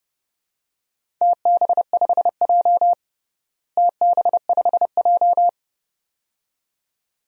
Morse code, heard as a keyed tone, about 30 wpm, 700 Hz. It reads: TB5J TB5J
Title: DXCC-100 Call Signs - 4 Characters - Repeated 2x - 30wpm